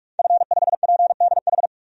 Morse code, heard as a keyed tone, about 41 wpm, 700 Hz. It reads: F5PDH
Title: DXCC-100 Call Signs - 5 Characters - 45wpm